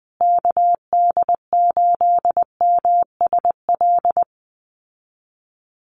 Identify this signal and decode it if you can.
Morse code, heard as a keyed tone, about 20 wpm, 700 Hz. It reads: KD8MSL
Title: USA Call Signs Repeated 3x 20wpm